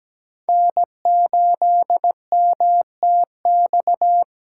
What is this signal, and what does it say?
Morse code, heard as a keyed tone, about 17 wpm, 700 Hz. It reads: N8MTX
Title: USA Callsign DE Callsign - 5 Characters - 17wpm